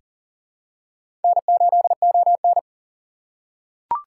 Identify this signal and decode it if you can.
Morse code, heard as a keyed tone, about 40 wpm, 700 Hz. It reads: N8ON
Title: USA Call Signs - 6x Character Spacing - 40wpm